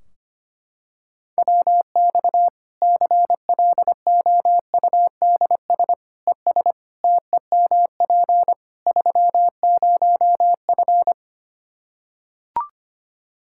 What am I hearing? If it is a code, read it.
Morse code, heard as a keyed tone, about 25 wpm, 700 Hz. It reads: WX CLOUDS ES TEMP 30F
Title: K1USN SST QSO Practice 25wpm